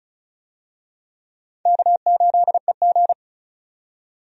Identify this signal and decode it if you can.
Morse code, heard as a keyed tone, about 36 wpm, 700 Hz. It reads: K8EG
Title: USA Call Signs - 4 Characters - 35wpm